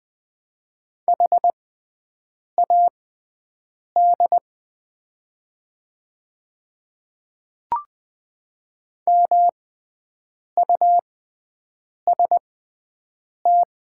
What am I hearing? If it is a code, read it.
Morse code, heard as a keyed tone, about 20 wpm, spaced as for 7 wpm, 700 Hz. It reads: HAD MUST
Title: 6x Character Spacing - Top 300 Words - 20wpm